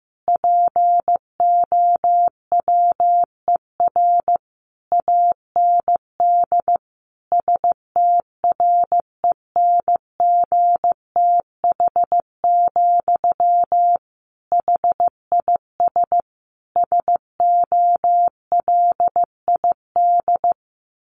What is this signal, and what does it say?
Morse code, heard as a keyed tone, about 15 wpm, 700 Hz. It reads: POWER AND STRENGTH, HIS SOLID
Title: Sentences from Top 1500 Words 15wpm